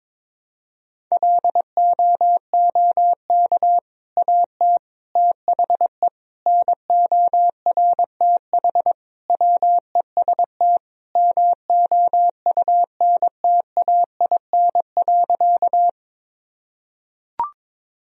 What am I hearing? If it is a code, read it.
Morse code, heard as a keyed tone, about 22 wpm, 700 Hz. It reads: LOOK AT THE NORTH WEST MOUNTAIN.
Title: Sentences from Top 1000 Words - Review Part 2 - 22wpm